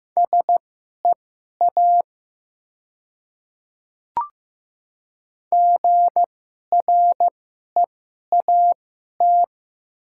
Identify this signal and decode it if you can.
Morse code, heard as a keyed tone, about 15 wpm, spaced as for 11 wpm, 700 Hz. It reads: SEA GREAT
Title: 2x Character Spacing - Top 500 Words - 15wpm